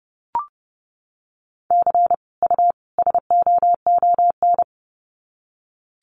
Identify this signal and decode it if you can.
Morse code, heard as a keyed tone, about 30 wpm, 700 Hz. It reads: C U SOON
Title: Common QSO Phrases 30wpm